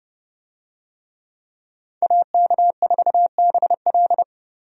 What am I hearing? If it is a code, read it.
Morse code, heard as a keyed tone, about 30 wpm, 700 Hz. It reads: AK4BL
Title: USA Call Signs - 5 Characters - 30wpm